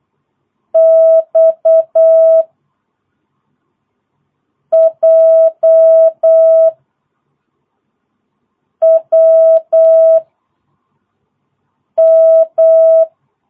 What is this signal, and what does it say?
Morse code, heard as a keyed tone, about 8 wpm, 660 Hz. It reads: X J W M
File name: jh8cbh / 2022cw